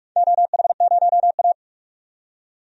Morse code, heard as a keyed tone, about 44 wpm, 700 Hz. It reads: OH0A
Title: DXCC-100 Call Signs - 4 Characters - 45wpm